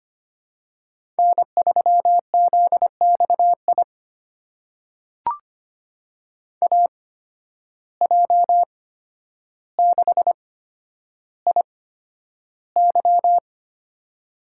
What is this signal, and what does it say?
Morse code, heard as a keyed tone, about 25 wpm, spaced as for 11 wpm, 700 Hz. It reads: N3ZXI AJ6IY
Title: USA Call Signs - 5 Characters - 8x Character Spacing - 25wpm